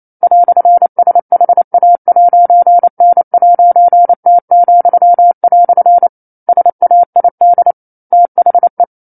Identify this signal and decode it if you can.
Morse code, heard as a keyed tone, about 28 wpm, 700 Hz. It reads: "SHA'N'T," SAID THE
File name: 09-Chapter-09 28 WPM